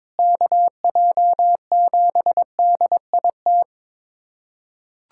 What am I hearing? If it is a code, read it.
Morse code, heard as a keyed tone, about 22 wpm, 700 Hz. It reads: KJ7DIT
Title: USA Callsign DE Callsign - 6 Characters - 22wpm